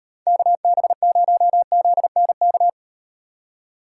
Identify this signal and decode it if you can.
Morse code, heard as a keyed tone, about 37 wpm, 700 Hz. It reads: KB0ZNK